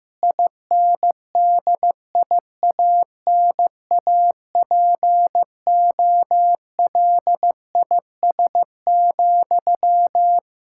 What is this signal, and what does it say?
Morse code, heard as a keyed tone, about 15 wpm, 700 Hz. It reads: INDIANAPOLIS,
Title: Practice QSO - Exchange 1 - Beginning - 15wpm